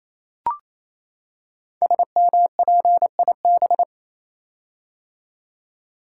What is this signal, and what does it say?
Morse code, heard as a keyed tone, about 29 wpm, 700 Hz. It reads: SMPIB
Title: Code Groups - 5 Letters - 28wpm